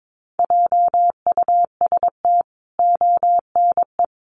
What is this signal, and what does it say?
Morse code, heard as a keyed tone, about 22 wpm, 700 Hz. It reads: JUST ONE
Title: Sentences from Top 1000 Words 22wpm